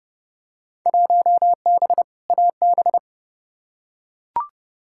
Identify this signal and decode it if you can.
Morse code, heard as a keyed tone, about 30 wpm, 700 Hz. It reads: 1B AB
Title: Field Day Exchanges - 30wpm